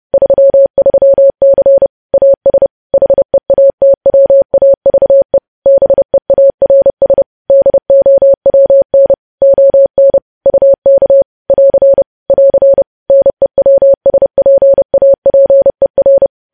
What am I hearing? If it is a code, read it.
Morse code, heard as a keyed tone, about 30 wpm, 550 Hz. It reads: 33C AS HEATWAVE BEARS DOWN ON UK <AR> <AR> NEWSPAPER
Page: news headlines in morse code at 30 wpm